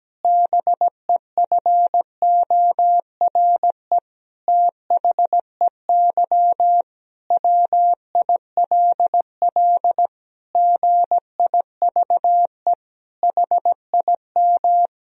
Morse code, heard as a keyed tone, about 17 wpm, 700 Hz. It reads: BEFORE THEY WILL GIVE HIM
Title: Sentences from Top 1000 Words - Review Part 4 - 17wpm